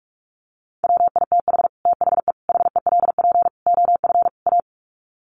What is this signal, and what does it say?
Morse code, heard as a keyed tone, about 44 wpm, 700 Hz. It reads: WITH THE HELP OF A
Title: Sets of 5 Words 45wpm